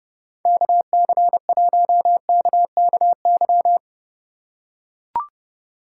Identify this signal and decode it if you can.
Morse code, heard as a keyed tone, about 30 wpm, 700 Hz. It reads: KC1KKY